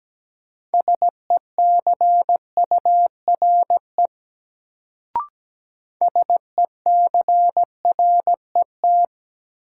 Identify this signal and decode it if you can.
Morse code, heard as a keyed tone, about 17 wpm, 700 Hz. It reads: SECURE SECRET
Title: Sentences from Top 1200 Words - New Word Drill Alt - 17wpm